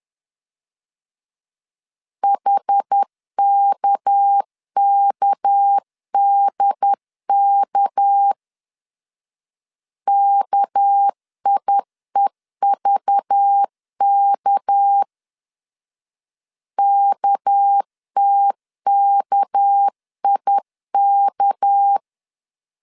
Morse code, heard as a keyed tone, about 10 wpm, 800 Hz. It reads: HKKDK KIEVK KTKIK
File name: LEZIONI / 6 PKX0 / 6 PK 10wpm 2436